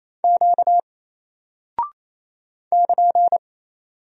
Morse code, heard as a keyed tone, about 28 wpm, 700 Hz. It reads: Q <KN>
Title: Single Character (Letter:Number:Punctuation:Prosign) 28wpm